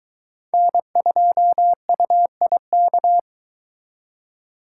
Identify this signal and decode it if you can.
Morse code, heard as a keyed tone, about 22 wpm, 700 Hz. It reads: N2UIK